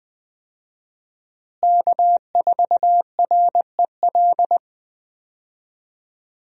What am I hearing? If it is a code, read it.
Morse code, heard as a keyed tone, about 20 wpm, 700 Hz. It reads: K4REL
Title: USA Call Signs - 5 Characters - Repeated 3x - 20wpm